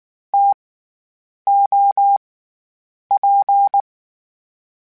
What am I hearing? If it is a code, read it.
Morse code, heard as a keyed tone, about 19 wpm, 800 Hz. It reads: T O P
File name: Variable Spacing / Variable Spacing-mp3s / TOs @18